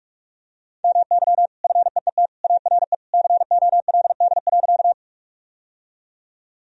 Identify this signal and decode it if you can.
Morse code, heard as a keyed tone, about 46 wpm, 700 Hz. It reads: MY FEET ARE COLD.
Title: Sentences from Top 1000 Words - Review Part 1 - 45wpm